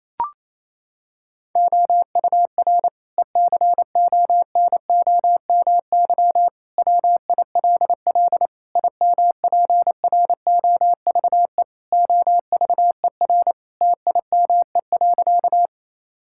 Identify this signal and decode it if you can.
Morse code, heard as a keyed tone, about 28 wpm, 700 Hz. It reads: OUR ECONOMY WILL IMPROVE OVER TIME.
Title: Sentences from Top 1000 Words - Review Part 5 - 28wpm